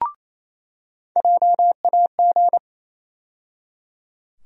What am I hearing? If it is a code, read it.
Morse code, heard as a keyed tone, about 28 wpm, 700 Hz. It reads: JAG
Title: USA Call Sign Postfixes - 3 Letters 28wpm